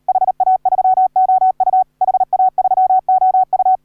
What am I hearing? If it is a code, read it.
Morse code, heard as a keyed tone, about 38 wpm, 750 Hz. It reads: HA3OU HA3OU
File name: AUDIO2019 / HA3OU CW